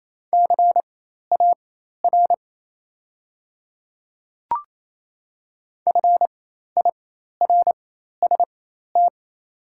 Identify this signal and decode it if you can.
Morse code, heard as a keyed tone, about 27 wpm, spaced as for 13 wpm, 700 Hz. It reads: CAR FIRST
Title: 4x Character Spacing - Top 300 Words - 28wpm